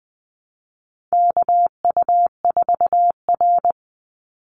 Morse code, heard as a keyed tone, about 20 wpm, 700 Hz. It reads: KU4R
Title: NA SOTA Chaser Callsigns 20wpm